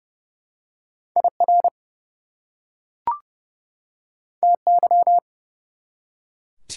Morse code, heard as a keyed tone, about 30 wpm, 700 Hz. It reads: IR TY
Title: Prefixes and Suffixes 30wpm